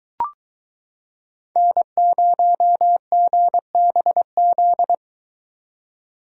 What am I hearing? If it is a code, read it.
Morse code, heard as a keyed tone, about 22 wpm, 700 Hz. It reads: N0GBZ